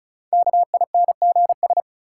Morse code, heard as a keyed tone, about 36 wpm, 700 Hz. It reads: KINGS
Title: Sentences from Top 1000 Words - Review Part 6 - New Word Drill - 35wpm